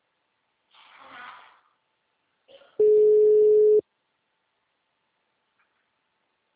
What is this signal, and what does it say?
Ringback tone (Europe, 425 Hz).